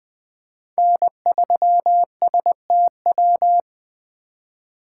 Morse code, heard as a keyed tone, about 20 wpm, 700 Hz. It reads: N3STW
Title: USA Call Signs - 5 Characters - 3x Character Spacing - 20wpm